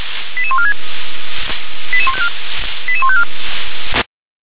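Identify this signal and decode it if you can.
Five-tone selective call (ZVEI), heard as quick runs of five tones, tones of about 70 ms. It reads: 88146 88154
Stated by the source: ZVEI five-tone call received by SDR.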